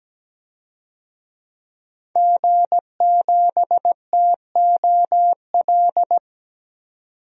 Morse code, heard as a keyed tone, about 17 wpm, 700 Hz. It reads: G7TOL
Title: DXCC-100 Call Signs - 5 Characters - 17wpm